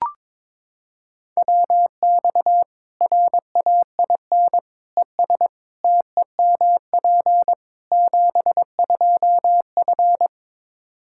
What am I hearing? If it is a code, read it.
Morse code, heard as a keyed tone, about 22 wpm, 700 Hz. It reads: WX RAIN ES TEMP 72F
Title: K1USN SST QSO Practice 22wpm